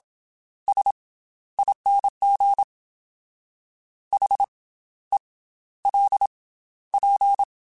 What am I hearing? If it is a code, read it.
Morse code, heard as a keyed tone, about 27 wpm, 800 Hz. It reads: S ING H E L P
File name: Special INGs / Special INGs-mp3s / Slow ing @25